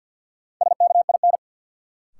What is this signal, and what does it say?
Morse code, heard as a keyed tone, about 50 wpm, 700 Hz. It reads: SKIN
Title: Sentences from Top 1300 Words - New Word Drill Alt - 50wpm